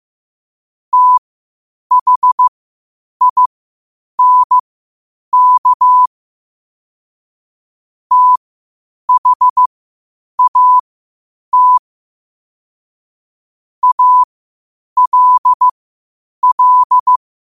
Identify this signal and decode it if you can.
Morse code, heard as a keyed tone, about 15 wpm, spaced as for 8 wpm, 1000 Hz. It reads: THINK THAT ALL
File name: qotdfiles / 8wpm